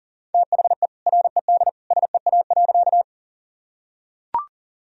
Morse code, heard as a keyed tone, about 40 wpm, 700 Hz. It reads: THE RED SEA.
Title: Sentences from Top 1000 Words - Review Part 2 - 40wpm